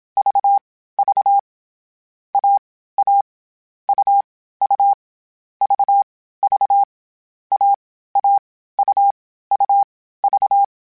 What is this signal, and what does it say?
Morse code, heard as a keyed tone, about 27 wpm, 800 Hz. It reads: V V A A U U V V A A U U V
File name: CW Code Groups-mp3s / Sets mp3s / AUV AUV4 sp @25